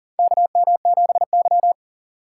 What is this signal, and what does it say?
Morse code, heard as a keyed tone, about 40 wpm, 700 Hz. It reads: KM7Y
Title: USA Call Signs - 4 Characters - 4x Character Spacing - 40wpm